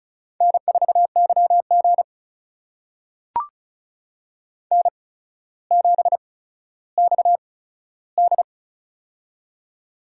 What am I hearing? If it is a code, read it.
Morse code, heard as a keyed tone, about 36 wpm, spaced as for 15 wpm, 700 Hz. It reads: N4YG N7XD
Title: USA Call Signs - 4 Characters - 8x Character Spacing - 35wpm